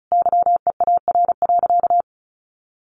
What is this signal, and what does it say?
Morse code, heard as a keyed tone, about 34 wpm, 700 Hz. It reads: YEAR.
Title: Sentences from Top 1300 Words 35wpm